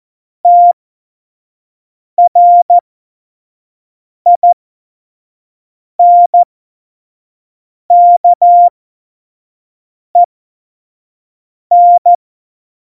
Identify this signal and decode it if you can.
Morse code, heard as a keyed tone, about 14 wpm, spaced as for 5 wpm, 700 Hz. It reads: TRINKEN
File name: Morsekurs / Witze / Witz 07. Januar 2020, 25 F70